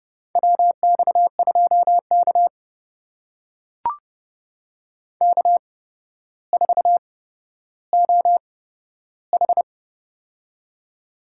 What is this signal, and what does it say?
Morse code, heard as a keyed tone, about 30 wpm, spaced as for 13 wpm, 700 Hz. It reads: WX2K K4OH
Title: USA Call Signs - 4 Characters - 8x Character Spacing - 30wpm